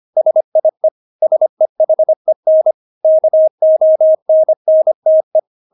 Morse code, heard as a keyed tone, about 25 wpm, 630 Hz. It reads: SIE SEHEN KONNTE
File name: Witz6 25wpm